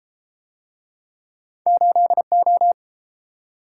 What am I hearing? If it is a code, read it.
Morse code, heard as a keyed tone, about 32 wpm, 700 Hz. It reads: 8O